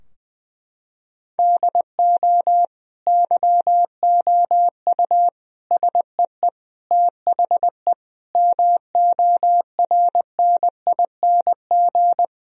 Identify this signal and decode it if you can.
Morse code, heard as a keyed tone, about 20 wpm, 700 Hz. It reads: DO YOU SEE THE MORNING
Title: Sentences from Top 1000 Words - Review Part 2 - 20wpm